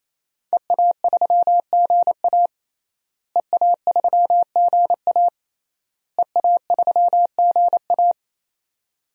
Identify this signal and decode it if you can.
Morse code, heard as a keyed tone, about 28 wpm, 700 Hz. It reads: EA3GA EA3GA EA3GA
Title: DXCC-100 Call Signs - 5 Characters - Repeated 3x - 28wpm